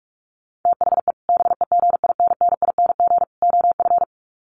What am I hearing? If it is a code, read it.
Morse code, heard as a keyed tone, about 44 wpm, 700 Hz. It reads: THE BEGINNING OF
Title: Sets of 3 Words 45wpm